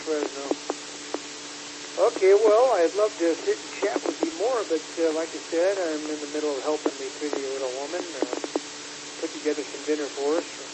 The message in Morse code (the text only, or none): none